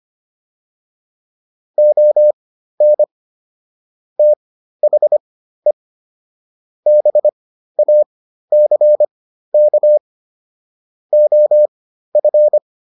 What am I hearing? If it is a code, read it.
Morse code, heard as a keyed tone, about 25 wpm, spaced as for 13 wpm, 600 Hz. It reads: ON THE BACK OF